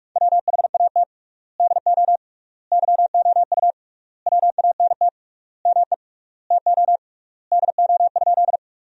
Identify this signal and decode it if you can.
Morse code, heard as a keyed tone, about 46 wpm, 700 Hz. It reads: WHAT DO YOU WANT ME TO DO?
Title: 3x Word-Spacing - Sentences from Top 200 Words - 45wpm